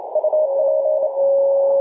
Morse code, heard as a keyed tone, about 27 wpm, 620 Hz. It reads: UP
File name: 3d2se-18-080mhz-cw